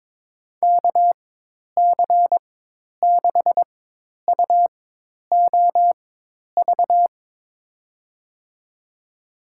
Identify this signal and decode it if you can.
Morse code, heard as a keyed tone, about 22 wpm, spaced as for 10 wpm, 700 Hz. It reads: KC6UOV
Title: USA Call Signs - 6 Characters - 4x Character Spacing - 22wpm